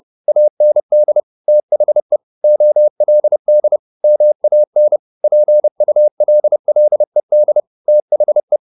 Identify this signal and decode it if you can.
Morse code, heard as a keyed tone, about 30 wpm, 600 Hz. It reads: AND THE OLD MAN PULLED THE